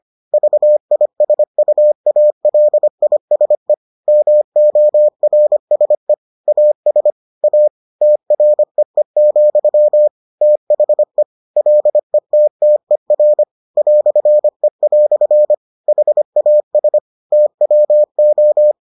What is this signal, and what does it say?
Morse code, heard as a keyed tone, about 25 wpm, 600 Hz. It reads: VISUALISE MORSE AS A TREE, THE LETTER "E" HAS TWO